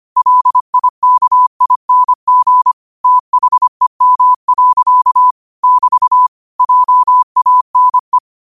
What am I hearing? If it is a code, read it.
Morse code, heard as a keyed tone, about 25 wpm, 1000 Hz. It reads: LIKING THEM. <BT> JANE
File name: qotdfiles / 25wpm